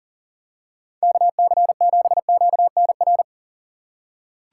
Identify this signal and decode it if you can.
Morse code, heard as a keyed tone, about 40 wpm, 700 Hz. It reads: KC7QNR
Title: USA Call Signs - 6 Characters - 6x Character Spacing - 40wpm